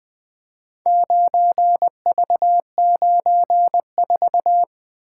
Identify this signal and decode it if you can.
Morse code, heard as a keyed tone, about 20 wpm, 700 Hz. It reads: 9V94
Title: Four Letter-Number Combinations - 20wpm